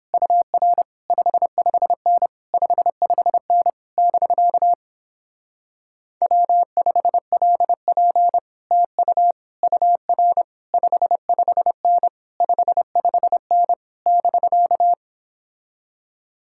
Morse code, heard as a keyed tone, about 29 wpm, 700 Hz. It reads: UR 55N 55N <BK> W5LP TU UR 55N 55N <BK>